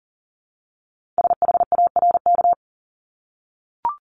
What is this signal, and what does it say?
Morse code, heard as a keyed tone, about 40 wpm, 700 Hz. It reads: SHARK
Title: Multi-letter Phonemes 40wpm